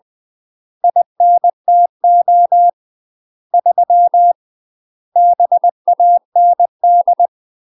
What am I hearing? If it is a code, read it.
Morse code, heard as a keyed tone, about 20 wpm, 700 Hz. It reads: INTO 3 BAND
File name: UK MASTER QSOs TRAINING / 20WPM / UK MASTER QSOs 0041